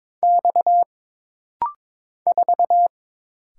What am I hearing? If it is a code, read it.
Morse code, heard as a keyed tone, about 22 wpm, 700 Hz. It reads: X 4
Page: This rapid-fire timing is geared to push your ICR (Instant Character Recognition) skills.